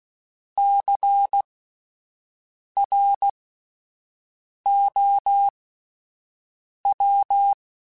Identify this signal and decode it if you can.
Morse code, heard as a keyed tone, about 16 wpm, spaced as for 5 wpm, 800 Hz. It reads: CROW
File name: Alphabetical / Alphabetical mp3s / Cs @5